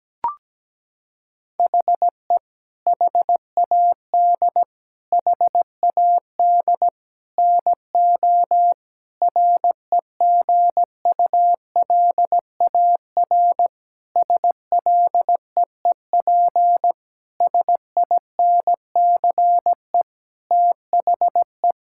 Morse code, heard as a keyed tone, about 17 wpm, 700 Hz. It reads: HE HAD HAD NO REGULAR SLEEP SINCE THE
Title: Sentences from Top 1000 Words - Review Part 6 - 17wpm